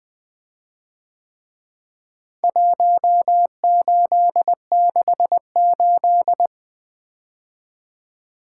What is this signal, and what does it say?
Morse code, heard as a keyed tone, about 23 wpm, 700 Hz. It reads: 1868
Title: Speed-Racing - Four Letter-Number Combinations - 20wpm